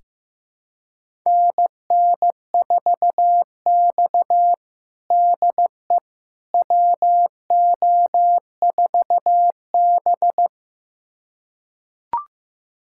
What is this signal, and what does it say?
Morse code, heard as a keyed tone, about 16 wpm, 700 Hz. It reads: NN4X DE WO4B
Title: USA Callsign DE Callsign - 4 Characters - 15wpm